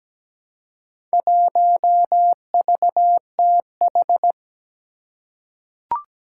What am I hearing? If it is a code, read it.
Morse code, heard as a keyed tone, about 17 wpm, 700 Hz. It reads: 1VTH
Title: Four Challenging Letter-Number Combinations 17wpm